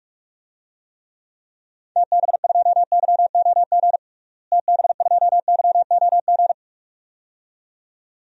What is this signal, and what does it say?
Morse code, heard as a keyed tone, about 46 wpm, 700 Hz. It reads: TB2YOG TB2YOG
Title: DXCC-100 Call Signs - 6 Characters - Repeated 2x - 45wpm